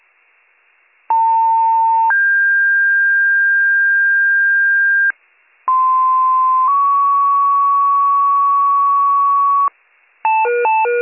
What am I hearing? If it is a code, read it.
Two-tone page, heard as a one-second tone, then a three-second tone, about 900 then 1600 Hz, about 1040 then 1120 Hz. It is heard from 1.1 to 9.7 s.